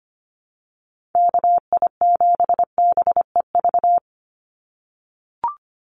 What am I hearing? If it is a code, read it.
Morse code, heard as a keyed tone, about 25 wpm, 700 Hz. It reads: KI7BEV
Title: USA Call Signs 25wpm